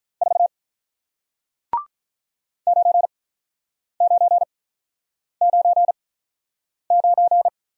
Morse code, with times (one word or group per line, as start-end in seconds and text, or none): 0.2-0.5 4
2.7-7.5 9999